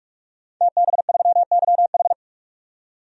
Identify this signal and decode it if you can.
Morse code, heard as a keyed tone, about 44 wpm, 700 Hz. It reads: TB3YH
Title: DXCC-100 Call Signs - 5 Characters - 45wpm